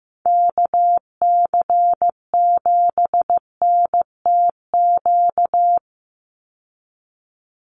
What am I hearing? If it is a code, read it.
Morse code, heard as a keyed tone, about 15 wpm, 700 Hz. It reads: KC7NTQ
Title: USA Call Signs 15wpm